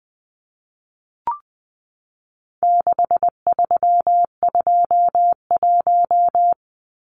Morse code, heard as a keyed tone, about 20 wpm, 700 Hz. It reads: 6321
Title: Numbers - 4-Digit 20wpm